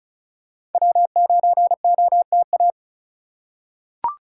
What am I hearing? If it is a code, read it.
Morse code, heard as a keyed tone, about 36 wpm, 700 Hz. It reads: W9OTA
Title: USA Call Signs - 5 Characters - 2x Character Spacing - 35wpm